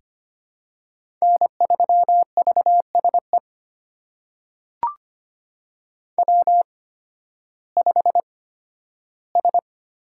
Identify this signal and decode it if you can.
Morse code, heard as a keyed tone, about 25 wpm, spaced as for 13 wpm, 700 Hz. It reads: N3VSE W5S
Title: USA Call Signs - 5 Characters - 8x Character Spacing - 25wpm